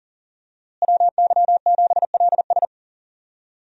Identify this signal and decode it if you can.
Morse code, heard as a keyed tone, about 40 wpm, 700 Hz. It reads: WY7LS
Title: USA Call Signs - 5 Characters - Repeated 2x - 40wpm